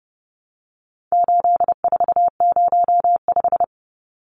Morse code, heard as a keyed tone, about 30 wpm, 700 Hz. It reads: 8405
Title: Numbers - 4-Digit 30wpm